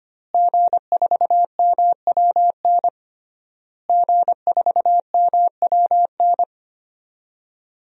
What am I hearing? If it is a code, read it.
Morse code, heard as a keyed tone, about 25 wpm, 700 Hz. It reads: G4MWN G4MWN
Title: DXCC-100 Call Signs - 5 Characters - Repeated 3x - 25wpm